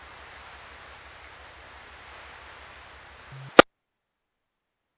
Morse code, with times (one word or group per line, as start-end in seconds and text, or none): none